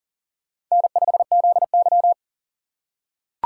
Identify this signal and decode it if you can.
Morse code, heard as a keyed tone, about 40 wpm, 700 Hz. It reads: N5ZY
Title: USA Call Signs - 4 Characters - 4x Character Spacing - 40wpm